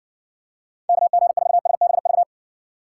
Morse code, heard as a keyed tone, about 60 wpm, 700 Hz. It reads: KG4SBV